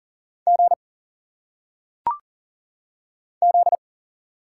Morse code, heard as a keyed tone, about 40 wpm, 700 Hz. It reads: G Z
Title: Single Letters - Phonetic Pronunciation 40wpm